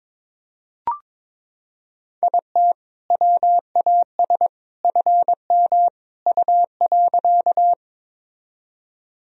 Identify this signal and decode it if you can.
Morse code, heard as a keyed tone, about 22 wpm, 700 Hz. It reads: IT WAS FM U.
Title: Sentences from Top 100 Words and Abbreviations 22wpm